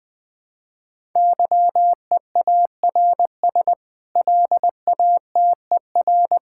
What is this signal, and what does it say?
Morse code, heard as a keyed tone, about 20 wpm, 700 Hz. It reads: YEARS LATER
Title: Sentences with 8-Word Phrases - 20wpm